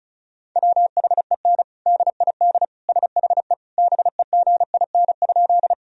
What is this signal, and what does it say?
Morse code, heard as a keyed tone, about 36 wpm, 700 Hz. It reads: WHEN DID SHE BEGIN?
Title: Sentences from Top 1000 Words - Review Part 1 - 35wpm